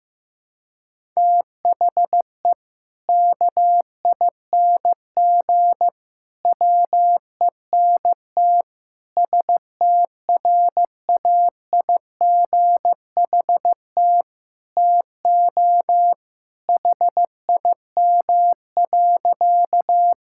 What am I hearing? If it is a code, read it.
Morse code, heard as a keyed tone, about 15 wpm, 700 Hz. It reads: THE KING WENT STRAIGHT TO HIM.
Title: Sentences from Top 1000 Words - Review Part 6 - 15wpm